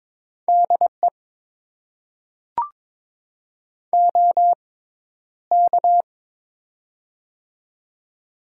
Morse code, heard as a keyed tone, about 22 wpm, spaced as for 11 wpm, 700 Hz. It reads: DE OK
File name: Top 90 QSO Elements - 6x Character Spacing - 22wpm